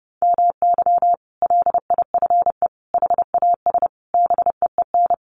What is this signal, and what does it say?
Morse code, heard as a keyed tone, about 28 wpm, 700 Hz. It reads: MY LIFE HAS BEEN
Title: Sentences from Top 1100 Words 30wpm